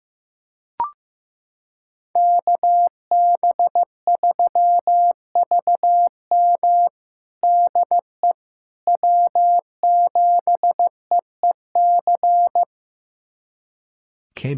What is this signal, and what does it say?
Morse code, heard as a keyed tone, about 15 wpm, 700 Hz. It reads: KB3VM DE W7EEC
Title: USA Callsign DE Callsign - 5 Characters - 15wpm